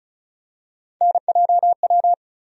Morse code, heard as a keyed tone, about 34 wpm, 700 Hz. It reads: NJW
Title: USA Call Sign Postfixes - 3 Letters 35wpm